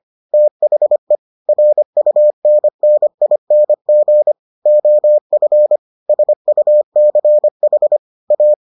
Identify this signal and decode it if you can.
Morse code, heard as a keyed tone, about 25 wpm, 600 Hz. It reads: THE RUNNING OF SUCH A